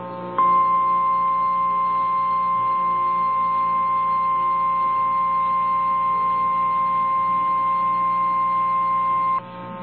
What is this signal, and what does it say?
Emergency Alert System (SAME) tones (the 1050 Hz attention tone). It is heard from 0.3 to 9.4 s.
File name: eas / KNCYKBIE / KNCYKBIE